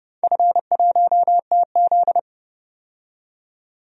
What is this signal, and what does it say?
Morse code, heard as a keyed tone, about 30 wpm, 700 Hz. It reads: F1TZ
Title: DXCC-100 Call Signs - 4 Characters - 30wpm